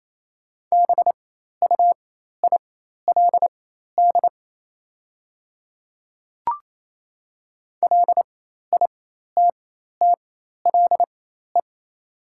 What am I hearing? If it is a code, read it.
Morse code, heard as a keyed tone, about 27 wpm, spaced as for 13 wpm, 690 Hz. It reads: BUILD LITTLE
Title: 4x Character Spacing - Top 200 Words - 28wpm